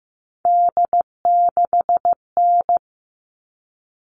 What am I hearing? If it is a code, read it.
Morse code, heard as a keyed tone, about 15 wpm, 700 Hz. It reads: D6N
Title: All Cut Numbers - 3-Digit 15wpm